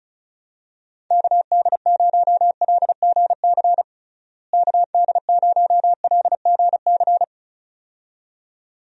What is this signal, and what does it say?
Morse code, heard as a keyed tone, about 36 wpm, 700 Hz. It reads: KD0LGC KD0LGC
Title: USA Call Signs - 6 Characters - Repeated 2x - 35wpm